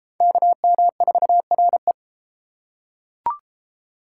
Morse code, heard as a keyed tone, about 31 wpm, 700 Hz. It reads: KM4RE